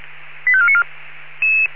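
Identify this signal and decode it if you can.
Five-tone selective call (ZVEI), heard as quick runs of five tones, tones of about 70 ms. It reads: 85494
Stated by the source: ZVEI five-tone call 85492 received on SDR.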